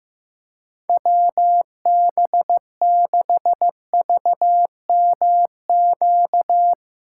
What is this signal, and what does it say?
Morse code, heard as a keyed tone, about 15 wpm, 700 Hz. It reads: WB6VMQ
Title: USA Callsign DE Callsign - 6 Characters - 15wpm